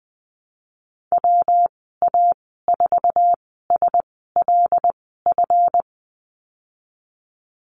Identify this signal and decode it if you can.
Morse code, heard as a keyed tone, about 20 wpm, spaced as for 15 wpm, 700 Hz. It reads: WA4SLF
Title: 173 Lesson 28g - Number 9 - Call Signs 20wpm